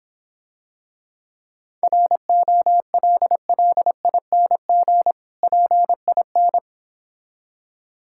Morse code, heard as a keyed tone, about 26 wpm, 690 Hz. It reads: ROLLING PIN